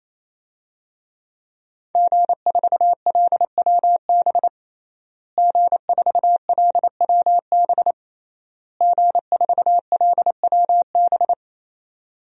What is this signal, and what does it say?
Morse code, heard as a keyed tone, about 28 wpm, 700 Hz. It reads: G4LWB G4LWB G4LWB
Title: DXCC-100 Call Signs - 5 Characters - Repeated 3x - 28wpm